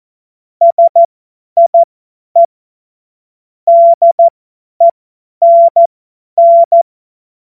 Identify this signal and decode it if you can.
Morse code, heard as a keyed tone, about 14 wpm, spaced as for 10 wpm, 680 Hz. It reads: SIE DENN
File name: Morsekurs / Witze / Witz 27. Januar 2026 50 F70